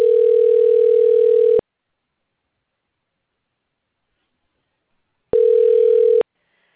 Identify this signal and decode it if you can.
Ringback tone (North America), 440 + 480 Hz.
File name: recordings / REc49fe390d507f6dd3ad2e0615bdd2c2e